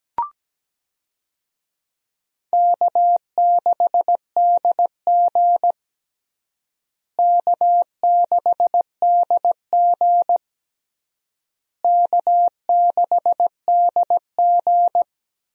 Morse code, heard as a keyed tone, about 17 wpm, 700 Hz. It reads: K6DG K6DG K6DG
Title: USA Call Signs - 4 Characters - Repeated 3x - 17wpm